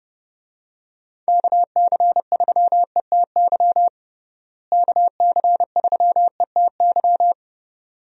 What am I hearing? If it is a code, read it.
Morse code, heard as a keyed tone, about 30 wpm, 700 Hz. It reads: KC3ETY KC3ETY
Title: USA Call Signs - 6 Characters - Repeated 2x - 30wpm